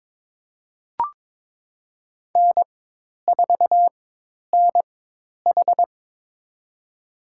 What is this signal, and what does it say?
Morse code, heard as a keyed tone, about 22 wpm, spaced as for 10 wpm, 700 Hz. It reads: N4NH
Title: USA Call Signs - 4x Character Spacing - 22wpm